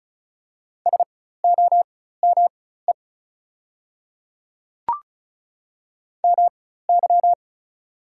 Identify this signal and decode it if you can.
Morse code, heard as a keyed tone, about 36 wpm, spaced as for 16 wpm, 700 Hz. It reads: SOME MY
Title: 4x Character Spacing - Top 200 Words - 35wpm